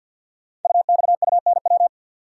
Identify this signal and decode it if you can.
Morse code, heard as a keyed tone, about 48 wpm, 700 Hz. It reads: UXUNW
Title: Code Groups - 5 Letters - 50wpm